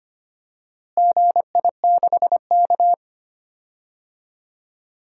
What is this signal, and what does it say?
Morse code, heard as a keyed tone, about 25 wpm, 700 Hz. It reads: GI6K
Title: DXCC-100 Call Signs - 4 Characters - Repeated 2x - 25wpm